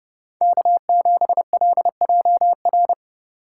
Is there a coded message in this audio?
Yes: Morse code from 0.4 to 2.9 s, heard as a keyed tone, about 34 wpm, 700 Hz. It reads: K7LJR